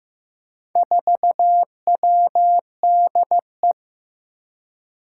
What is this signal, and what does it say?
Morse code, heard as a keyed tone, about 15 wpm, 700 Hz. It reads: 4WDE
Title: Four Challenging Letter-Number Combinations 15wpm